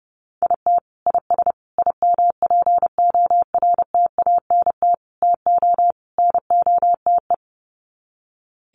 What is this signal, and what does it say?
Morse code, heard as a keyed tone, about 30 wpm, 700 Hz. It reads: IT IS IMPORTANT TO NOTE
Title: Sets of 5 Words 30wpm